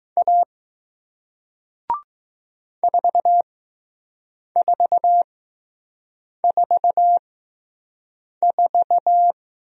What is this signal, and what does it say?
Morse code, heard as a keyed tone, about 21 wpm, spaced as for 6 wpm, 700 Hz. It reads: A 4444